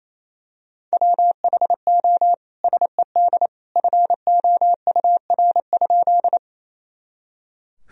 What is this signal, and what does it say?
Morse code, heard as a keyed tone, about 28 wpm, 700 Hz. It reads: WHO SED FOUR?
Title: Sentences from Top 300 Words and Abbreviations 28wpm